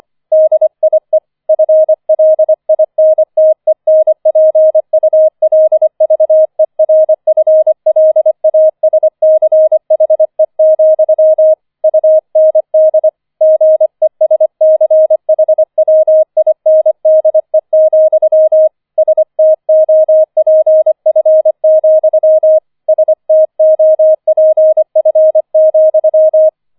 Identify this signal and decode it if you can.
Morse code, heard as a keyed tone, about 25 wpm, 630 Hz. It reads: DIE FLINTENPULVERFLASCHE, UND GESCHWINDE, STOPF, STOPF,
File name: 004 Vierter Streich 25wpm